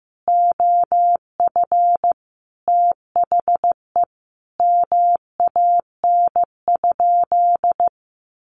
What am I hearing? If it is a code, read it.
Morse code, heard as a keyed tone, about 15 wpm, 700 Hz. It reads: OF THE MAN?
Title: Sentences from Top 1300 Words 15wpm